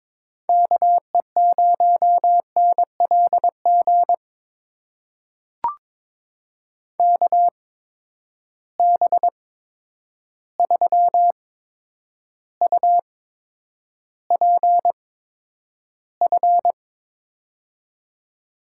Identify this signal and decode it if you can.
Morse code, heard as a keyed tone, about 22 wpm, spaced as for 9 wpm, 700 Hz. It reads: KE0NLG KB3UPF
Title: USA Call Signs - 6 Characters - 8x Character Spacing - 22wpm